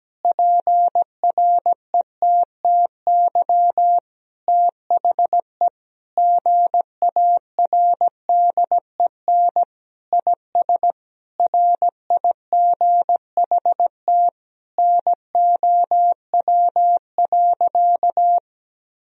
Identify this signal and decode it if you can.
Morse code, heard as a keyed tone, about 17 wpm, 700 Hz. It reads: PRETTY THE GARDEN IS RIGHT NOW.
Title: Sentences from Top 1000 Words - Review Part 4 - 17wpm